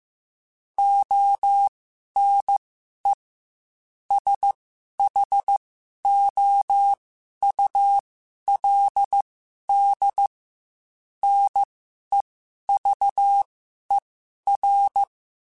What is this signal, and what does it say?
Morse code, heard as a keyed tone, about 15 wpm, spaced as for 11 wpm, 800 Hz. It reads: ONE SHOULD NEVER